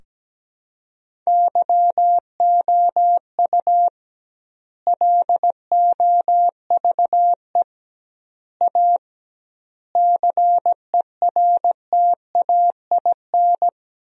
Morse code, heard as a keyed tone, about 17 wpm, 700 Hz. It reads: YOU LOVE A CERTAIN
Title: 2x Word-Spacing - Sentences from Top 500 Words - 17wpm